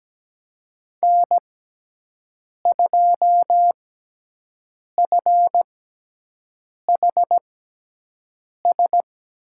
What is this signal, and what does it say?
Morse code, heard as a keyed tone, about 17 wpm, spaced as for 6 wpm, 700 Hz. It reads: N2FHS
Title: USA Call Signs - 5 Characters - 6x Character Spacing - 17wpm